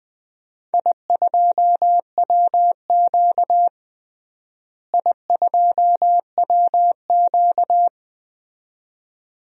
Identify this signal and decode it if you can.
Morse code, heard as a keyed tone, about 20 wpm, 700 Hz. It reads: I2WQ I2WQ
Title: DXCC-100 Call Signs - 4 Characters - Repeated 2x - 20wpm